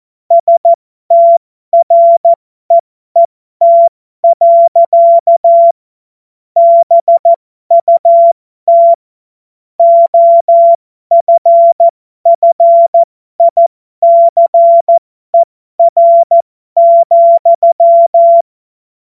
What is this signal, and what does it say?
Morse code, heard as a keyed tone, about 14 wpm, spaced as for 12 wpm, 680 Hz. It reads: STREET. BUT OFFICER,
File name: Morsekurs / Witze